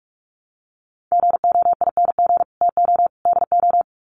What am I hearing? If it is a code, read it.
Morse code, heard as a keyed tone, about 45 wpm, 700 Hz. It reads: GOING TO DO
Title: Sets of 3 Words 45wpm